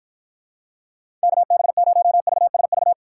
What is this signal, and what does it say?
Morse code, heard as a keyed tone, about 50 wpm, 700 Hz. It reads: KB0VSV